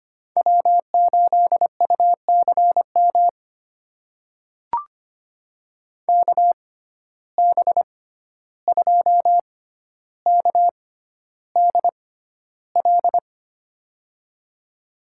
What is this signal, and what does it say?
Morse code, heard as a keyed tone, about 25 wpm, spaced as for 12 wpm, 700 Hz. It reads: W8UCM KB2KDL
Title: USA Call Signs - 6x Character Spacing - 25wpm